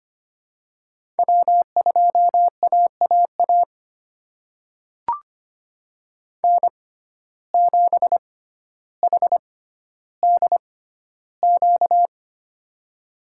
Morse code, heard as a keyed tone, about 25 wpm, spaced as for 13 wpm, 700 Hz. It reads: W2AAA N7HDQ
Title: USA Call Signs - 5 Characters - 6x Character Spacing - 25wpm